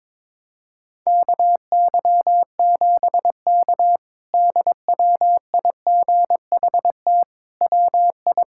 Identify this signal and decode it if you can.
Morse code, heard as a keyed tone, about 22 wpm, 700 Hz. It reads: KY7K DWIGHT WI
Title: K1USN SST QSO Practice - 22wpm